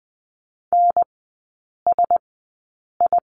Morse code, heard as a keyed tone, about 20 wpm, spaced as for 8 wpm, 700 Hz. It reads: NSI
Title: 041 Lesson 8e - Number 1 - Three Characters 20wpm